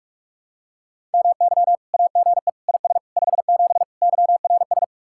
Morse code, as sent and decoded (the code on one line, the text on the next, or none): -- -.-- / .- --. / .. ... / ..... --... / -.-- .-. ...
MY AGE IS 57 YRS